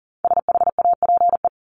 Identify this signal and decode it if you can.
Morse code, heard as a keyed tone, about 40 wpm, 700 Hz. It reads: SHAPE
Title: Top 500 Words 40wpm